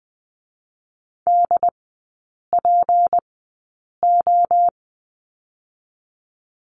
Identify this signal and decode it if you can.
Morse code, heard as a keyed tone, about 20 wpm, 700 Hz. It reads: D P O
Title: 212 Lesson 34e - Number 8 - Three Characters 20wpm